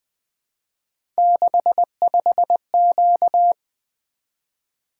Morse code, heard as a keyed tone, about 20 wpm, 700 Hz. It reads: 65Q
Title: Three Letter-Number Combinations - 20wpm